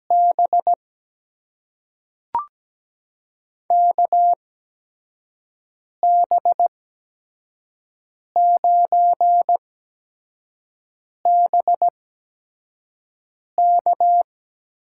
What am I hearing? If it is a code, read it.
Morse code, heard as a keyed tone, about 17 wpm, spaced as for 5 wpm, 700 Hz. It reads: B KB9BK
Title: USA Call Signs - 6 Characters - 8x Character Spacing - 17wpm